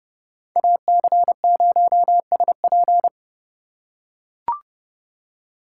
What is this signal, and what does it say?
Morse code, heard as a keyed tone, about 30 wpm, 700 Hz. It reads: AC0SP
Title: USA Call Signs - 5 Characters - 4x Character Spacing - 30wpm